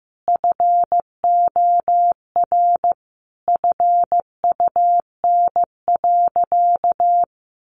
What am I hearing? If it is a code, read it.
Morse code, heard as a keyed tone, about 15 wpm, 700 Hz. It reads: FOR FUN.
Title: Easy Phrases 15wpm